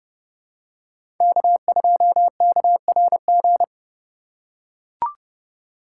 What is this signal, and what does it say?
Morse code, heard as a keyed tone, about 30 wpm, 700 Hz. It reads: K2KRG
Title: USA Call Signs - 5 Characters - 8x Character Spacing - 30wpm